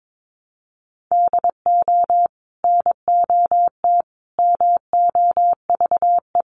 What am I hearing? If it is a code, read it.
Morse code, heard as a keyed tone, about 22 wpm, 700 Hz. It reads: DO NOT MOVE
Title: Sentences from Top 1300 Words 22wpm